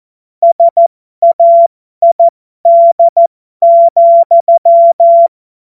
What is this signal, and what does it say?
Morse code, heard as a keyed tone, about 14 wpm, spaced as for 12 wpm, 680 Hz. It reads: SAID,